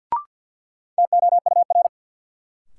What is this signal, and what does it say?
Morse code, heard as a keyed tone, about 50 wpm, 700 Hz. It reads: TOUR
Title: Sentences from Top 1400 Words - New Word Drill Alt - 50wpm